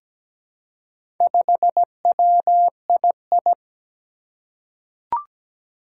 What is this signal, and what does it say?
Morse code, heard as a keyed tone, about 17 wpm, 700 Hz. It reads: 5WII
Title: Four Challenging Letter-Number Combinations 17wpm